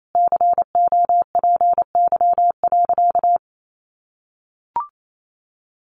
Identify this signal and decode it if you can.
Morse code, heard as a keyed tone, about 28 wpm, 700 Hz. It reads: COPY.
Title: Sentences from Top 1400 Words 28wpm